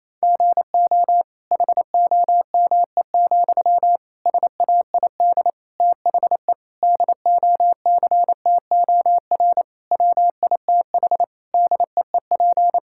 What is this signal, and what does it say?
Morse code, heard as a keyed tone, about 28 wpm, 700 Hz. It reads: GO HOME, SAID THE DOCTOR WITH DEEP
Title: Sentences from Top 1600 Words 28wpm